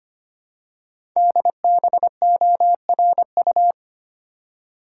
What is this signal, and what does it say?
Morse code, heard as a keyed tone, about 25 wpm, 700 Hz. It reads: DBORU
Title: Code Groups - 5 Letters - 25wpm